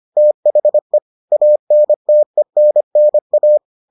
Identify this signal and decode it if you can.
Morse code, heard as a keyed tone, about 25 wpm, 600 Hz. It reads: THE ANTENNA